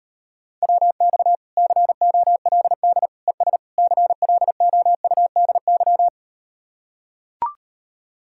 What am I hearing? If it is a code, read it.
Morse code, heard as a keyed tone, about 38 wpm, 700 Hz. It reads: WX COLD ES CLOUDY